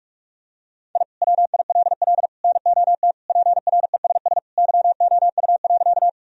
Morse code, heard as a keyed tone, about 41 wpm, 700 Hz. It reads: I WILL NOT PRESS YOU.
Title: Sentences from Top 1000 Words - Review Part 1 - 45wpm